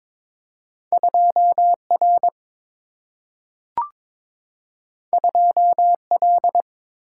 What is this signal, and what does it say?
Morse code, heard as a keyed tone, about 22 wpm, 700 Hz. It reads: 2 R 2 L
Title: Single-Letter Single-Number 22wpm